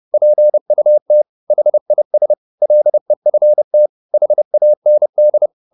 Morse code, heard as a keyed tone, about 30 wpm, 600 Hz. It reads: PUT HIS LEFT HAND